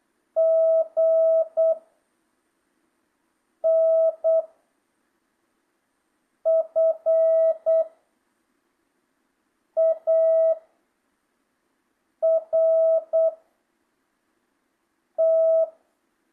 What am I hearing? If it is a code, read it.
Morse code, heard as a keyed tone, about 8 wpm, 660 Hz. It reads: G N F A R T
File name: jh8cbh / 2025cw